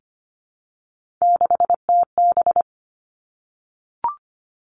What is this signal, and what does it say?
Morse code, heard as a keyed tone, about 25 wpm, 700 Hz. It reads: 6TB
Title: Three Challenging Letter-Number Combinations 25wpm